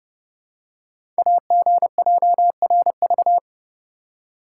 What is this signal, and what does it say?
Morse code, heard as a keyed tone, about 32 wpm, 700 Hz. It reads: AGJRV
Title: Code Groups - 5 Letter:Numbers - 30wpm